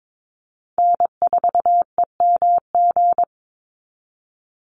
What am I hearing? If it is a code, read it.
Morse code, heard as a keyed tone, about 22 wpm, 700 Hz. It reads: N4EMG
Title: NA SOTA Chaser Callsigns 22wpm